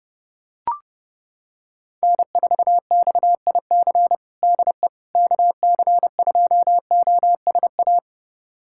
Morse code, heard as a keyed tone, about 30 wpm, 700 Hz. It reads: N4XIC DE KC2OSA
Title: USA Callsign DE Callsign 30wpm